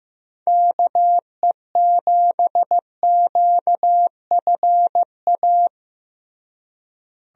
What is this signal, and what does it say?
Morse code, heard as a keyed tone, about 15 wpm, 700 Hz. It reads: KE7QFA
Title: USA Call Signs - 6 Characters - Repeated 2x - 15wpm